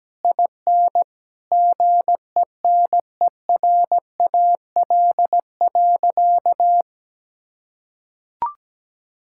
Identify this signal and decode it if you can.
Morse code, heard as a keyed tone, about 17 wpm, 700 Hz. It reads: IN GENERAL.
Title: Sentences from Top 1600 Words 17wpm